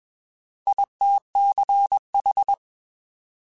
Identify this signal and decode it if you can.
Morse code, heard as a keyed tone, about 21 wpm, 800 Hz. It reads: ITCH